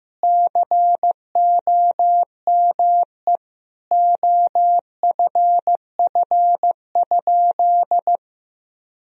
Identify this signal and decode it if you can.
Morse code, heard as a keyed tone, about 16 wpm, 700 Hz. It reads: COME OFF?
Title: Sentences from Top 1000 Words - Review Part 3 - 15wpm